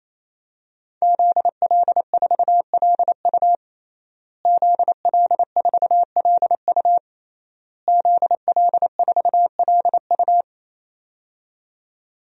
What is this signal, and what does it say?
Morse code, heard as a keyed tone, about 28 wpm, 700 Hz. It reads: ZL4LU ZL4LU ZL4LU
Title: DXCC-100 Call Signs - 5 Characters - Repeated 3x - 28wpm